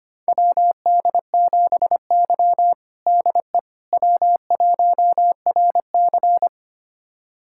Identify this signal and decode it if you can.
Morse code, heard as a keyed tone, about 25 wpm, 700 Hz. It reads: WD7Y DE W1RC
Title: USA Callsign DE Callsign - 4 Characters - 25wpm